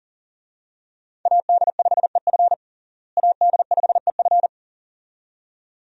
Morse code, heard as a keyed tone, about 40 wpm, 700 Hz. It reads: AD5EF AD5EF
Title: USA Call Signs - 5 Characters - Repeated 2x - 40wpm